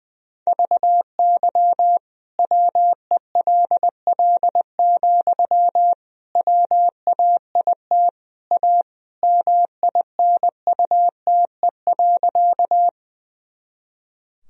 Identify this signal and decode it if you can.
Morse code, heard as a keyed tone, about 20 wpm, 700 Hz. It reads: VY WELL, WAIT A MINUTE.
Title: Sentences from Top 500 Words and Abbreviations 20wpm